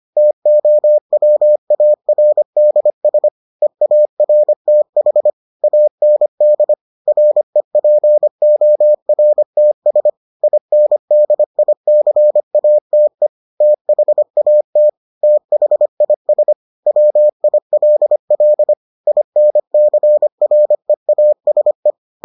Morse code, heard as a keyed tone, about 25 wpm, 600 Hz. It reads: TOWARDS EARTH AND REPORTS INDICATE THAT THIS WILL INCREASE